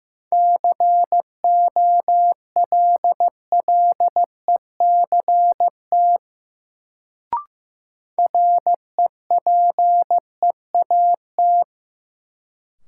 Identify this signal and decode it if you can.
Morse code, heard as a keyed tone, about 16 wpm, 700 Hz. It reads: COLLECT REPEAT
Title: Sentences from Top 1600 Words - New Word Drill - 15wpm